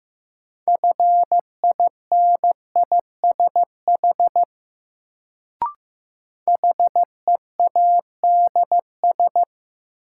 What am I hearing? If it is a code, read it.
Morse code, heard as a keyed tone, about 15 wpm, 700 Hz. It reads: FINISH HEADS
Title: Sentences from Top 1000 Words - Review Part 3 - New Word Drill - 15wpm